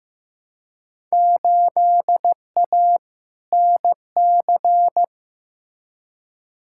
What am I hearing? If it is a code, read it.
Morse code, heard as a keyed tone, about 16 wpm, 700 Hz. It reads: 8A NC
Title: Field Day Exchanges - 15wpm